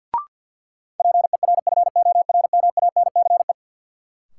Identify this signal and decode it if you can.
Morse code, heard as a keyed tone, about 52 wpm, 700 Hz. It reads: PERFORMANCE
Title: Sentences from Top 1000 Words - New Word Drill - 50wpm